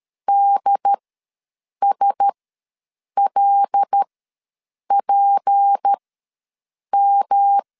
Morse code, heard as a keyed tone, about 13 wpm, spaced as for 7 wpm, 800 Hz. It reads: DSLPM
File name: LEZIONI / 8 YJ45 / 8 YJ4 ws 12wpm 3249